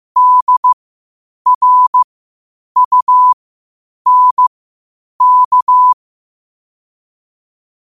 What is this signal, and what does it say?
Morse code, heard as a keyed tone, about 15 wpm, spaced as for 8 wpm, 1000 Hz. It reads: DRUNK
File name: qotdfiles / 8wpm